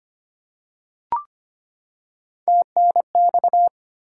Morse code, heard as a keyed tone, about 25 wpm, 700 Hz. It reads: TNX
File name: practice / 25wpm / Top 20 QSO Elements - 25wpm